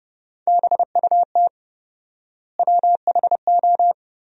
Morse code, heard as a keyed tone, about 30 wpm, 700 Hz. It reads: BUT WHO
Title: 4x Word-Spacing - Sentences from Top 200 Words - 30wpm